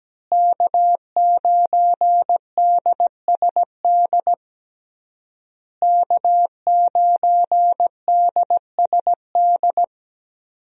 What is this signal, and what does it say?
Morse code, heard as a keyed tone, about 17 wpm, 700 Hz. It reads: K9DSD K9DSD
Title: USA Call Signs - 5 Characters - Repeated 3x - 17wpm